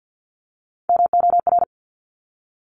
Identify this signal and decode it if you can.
Morse code, heard as a keyed tone, about 50 wpm, 700 Hz. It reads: MOR
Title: Top Three Letter Combinations in Words 50wpm